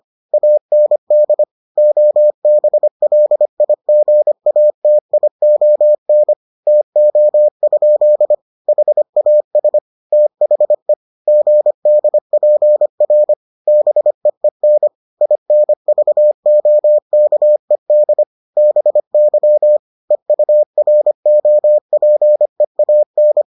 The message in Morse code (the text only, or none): AND OBLIGATION TO? HAS THE GDPR BEEN INVOKED BY EUROPEAN